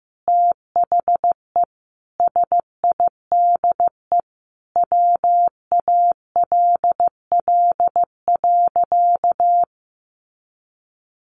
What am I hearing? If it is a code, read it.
Morse code, heard as a keyed tone, about 15 wpm, 700 Hz. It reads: THE SIDE WALL.
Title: Sentences from Top 1100 Words 15wpm